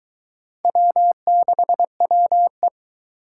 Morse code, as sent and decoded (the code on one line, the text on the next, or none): .-- -.... .-- .
W6WE